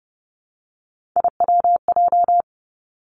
Morse code, heard as a keyed tone, about 30 wpm, 700 Hz. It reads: IWJ
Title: Three Challenging Letter-Number Combinations 30wpm